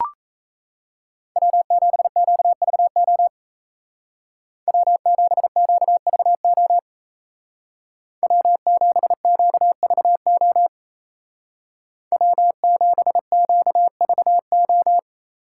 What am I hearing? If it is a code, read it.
Morse code, heard as a keyed tone, about 35 wpm, 700 Hz. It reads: W7QVO W7QVO W7QVO W7QVO
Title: Speed Racing - USA 1x3 Call Signs 28wpm